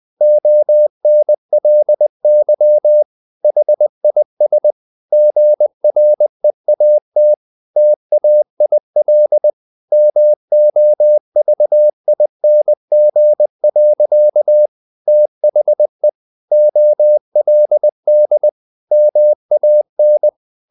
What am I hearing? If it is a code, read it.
Morse code, heard as a keyed tone, about 20 wpm, 600 Hz. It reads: ONLY HIS GREAT TAIL MOVING. THE OLD MAN
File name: the old man and the sea part 2 20wpm